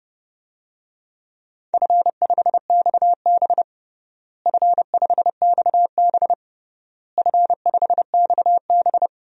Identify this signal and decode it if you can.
Morse code, heard as a keyed tone, about 29 wpm, 700 Hz. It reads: F5XB F5XB F5XB
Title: DXCC-100 Call Signs - 4 Characters - Repeated 3x - 30wpm